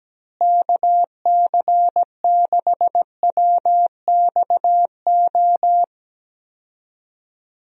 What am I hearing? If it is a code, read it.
Morse code, heard as a keyed tone, about 17 wpm, 700 Hz. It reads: KC6WXO
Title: USA Call Signs - 6 Characters - Repeated 2x - 17wpm